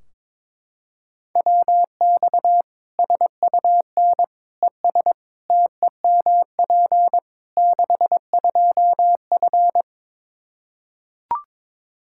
Morse code, heard as a keyed tone, about 22 wpm, 700 Hz. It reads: WX SUN ES TEMP 62F
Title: Practice QSO - Exchange 2 - Info - 22wpm